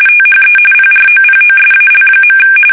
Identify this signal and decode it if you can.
RTTY, about 45.45 baud. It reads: THE PROPOSAL.
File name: fsk850 / usnavy3